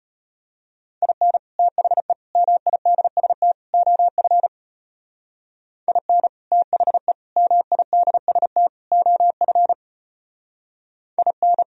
Morse code, heard as a keyed tone, about 33 wpm, 700 Hz. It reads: IN THE MIDST OF IN THE MIDST OF IN